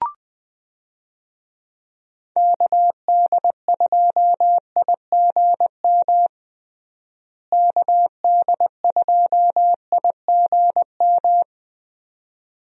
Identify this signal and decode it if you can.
Morse code, heard as a keyed tone, about 20 wpm, 700 Hz. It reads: KD2IGM KD2IGM
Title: USA Call Signs - 6 Characters - Repeated 2x - 20wpm